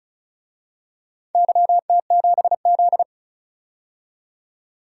Morse code, heard as a keyed tone, about 36 wpm, 700 Hz. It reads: YT7Z
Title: DXCC-100 Call Signs - 4 Characters - 35wpm